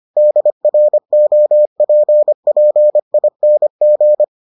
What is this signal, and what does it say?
Morse code, heard as a keyed tone, about 25 wpm, 600 Hz. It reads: DROPPING